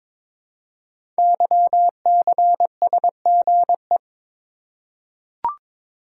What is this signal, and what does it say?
Morse code, heard as a keyed tone, about 22 wpm, 700 Hz. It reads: YCSGE
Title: Code Groups - 5 Letters - 22wpm